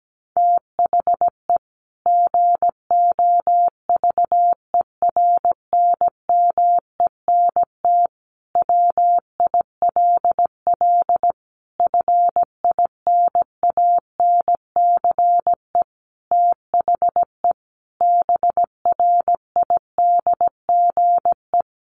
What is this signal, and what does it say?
Morse code, heard as a keyed tone, about 17 wpm, 700 Hz. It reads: THE GOVERNMENT WILL FINANCE THE BRIDGE
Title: Sentences from Top 1500 Words 17wpm